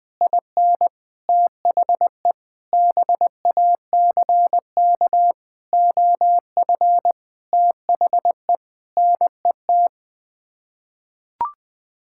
Morse code, heard as a keyed tone, about 20 wpm, 700 Hz. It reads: IN THE BACK OF THE NET
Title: Sets of 9 Words - 20wpm